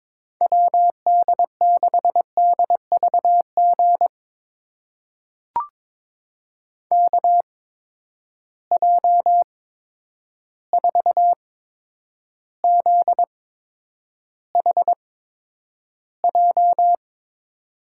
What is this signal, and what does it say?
Morse code, heard as a keyed tone, about 22 wpm, spaced as for 9 wpm, 700 Hz. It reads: WD6DVG KJ4ZHJ
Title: USA Call Signs - 6 Characters - 8x Character Spacing - 22wpm